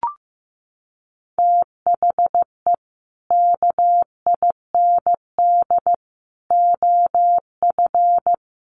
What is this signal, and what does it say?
Morse code, heard as a keyed tone, about 15 wpm, 700 Hz. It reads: THE KIND OF
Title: Sets of 3 Words 15wpm